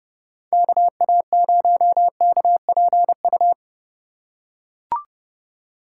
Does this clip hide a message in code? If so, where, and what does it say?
Yes: Morse code from 0.5 to 3.5 s, heard as a keyed tone, about 28 wpm, 700 Hz. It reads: KA0KPU